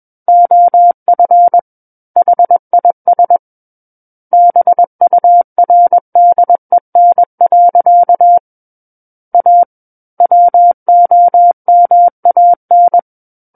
Morse code, heard as a keyed tone, about 21 wpm, 700 Hz. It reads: OF HIS BURDEN. A WOMAN
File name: Book One / 12-Chapter 12 - 21 WPM